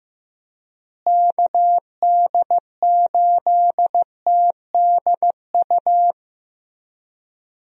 Morse code, heard as a keyed tone, about 15 wpm, 700 Hz. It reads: KD8TDU
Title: USA Call Signs - 6 Characters - Repeated 3x - 15wpm